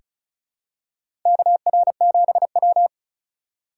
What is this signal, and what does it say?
Morse code, heard as a keyed tone, about 36 wpm, 700 Hz. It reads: KR7W
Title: K1USN SST QSO Practice - 35wpm